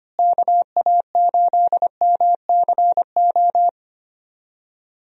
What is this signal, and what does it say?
Morse code, heard as a keyed tone, about 25 wpm, 700 Hz. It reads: KA8MCO
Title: USA Call Signs - 6 Characters - Repeated 3x - 25wpm